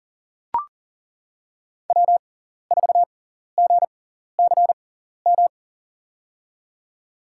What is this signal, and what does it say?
Morse code, heard as a keyed tone, about 40 wpm, spaced as for 14 wpm, 700 Hz. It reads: W4GCM
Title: USA Call Signs - 5 Characters - 6x Character Spacing - 40wpm